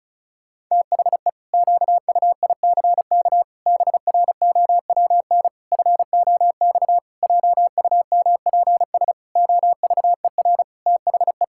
Morse code, heard as a keyed tone, about 36 wpm, 700 Hz. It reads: THE QUICK BROWN FOX JUMPS OVER THE
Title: Pangram Challenge v2 - 35wpm